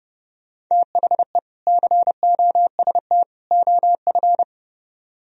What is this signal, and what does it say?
Morse code, heard as a keyed tone, about 37 wpm, 700 Hz. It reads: THE COST OF
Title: Speed-Racing - Sets of 3 Words 30wpm